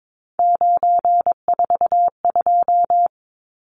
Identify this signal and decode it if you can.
Morse code, heard as a keyed tone, about 22 wpm, 700 Hz. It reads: 942
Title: Numbers - 3-Digit 22wpm